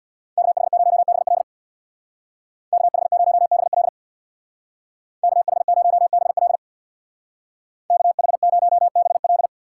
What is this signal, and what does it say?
Morse code, heard as a keyed tone, about 60 wpm, 700 Hz. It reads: KH0BL KH0BL KH0BL KH0BL
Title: Speed Racing - USA 2x2 Call Signs 50wpm